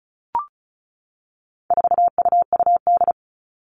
Morse code, heard as a keyed tone, about 34 wpm, 700 Hz. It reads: 4UUD
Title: All Cut Numbers - 4-Digit 35wpm